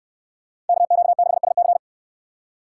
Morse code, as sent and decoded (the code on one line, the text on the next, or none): -.- --.- -.... ... -.-.
KQ6SC